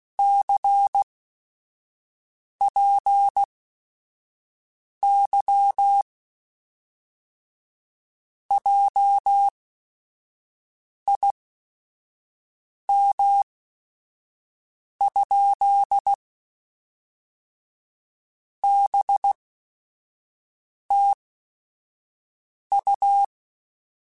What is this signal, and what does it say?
Morse code, heard as a keyed tone, about 16 wpm, spaced as for 5 wpm, 800 Hz. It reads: CPY JIM? BTU
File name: QSOs / QSOs-mp3s / WD0BGP 1@5